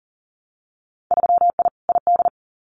Morse code, heard as a keyed tone, about 42 wpm, 700 Hz. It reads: 3I ID
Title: Winter Field Day Exchanges 40wpm